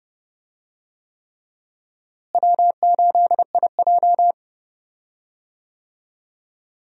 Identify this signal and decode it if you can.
Morse code, heard as a keyed tone, about 31 wpm, 700 Hz. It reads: W8IJ
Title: USA Call Signs - 4 Characters - Repeated 3x - 30wpm